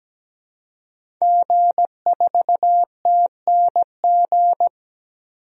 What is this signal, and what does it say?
Morse code, heard as a keyed tone, about 17 wpm, 700 Hz. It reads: G4TNG
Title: DXCC-100 Call Signs - 5 Characters - Repeated 2x - 17wpm